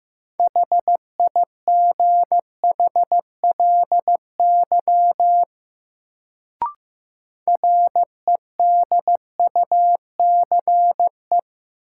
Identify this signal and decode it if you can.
Morse code, heard as a keyed tone, about 15 wpm, 700 Hz. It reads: HIGHLY REDUCE
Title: Sentences from Top 1100 Words - New Word Drill - 15wpm